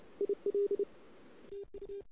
Morse code, heard as a keyed tone, about 29 wpm, 400 Hz. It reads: IL TU
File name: CW / 2008-ss-cw-k5tr-op-n5rz -00:00